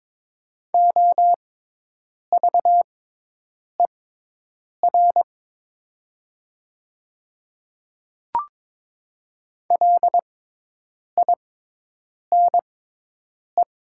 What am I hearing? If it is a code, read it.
Morse code, heard as a keyed tone, about 21 wpm, spaced as for 8 wpm, 700 Hz. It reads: OVER LINE
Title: 6x Character Spacing - Top 300 Words - 22wpm